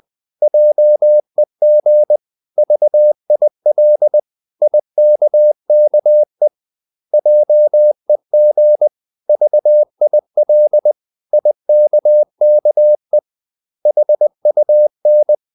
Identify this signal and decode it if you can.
Morse code, heard as a keyed tone, about 20 wpm, 600 Hz. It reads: JEG VIL IKKE JEG VIL IKKE HUN
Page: Ferie 19 wpm